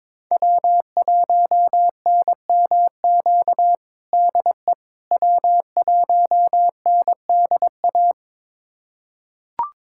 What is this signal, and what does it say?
Morse code, heard as a keyed tone, about 21 wpm, 700 Hz. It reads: W1NMQ DE W1NDA
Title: USA Callsign DE Callsign 22wpm